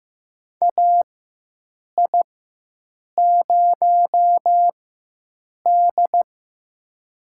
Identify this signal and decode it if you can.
Morse code, heard as a keyed tone, about 15 wpm, spaced as for 7 wpm, 700 Hz. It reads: AI0D
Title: USA Call Signs - 4 Characters - 4x Character Spacing - 15wpm